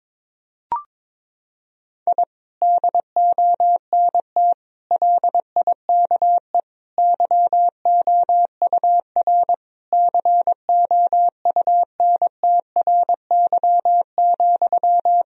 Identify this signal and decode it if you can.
Morse code, heard as a keyed tone, about 22 wpm, 700 Hz. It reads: I DONT LIKE YOUR COUNTRY,
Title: Sentences from Top 1000 Words - Review Part 3 - 22wpm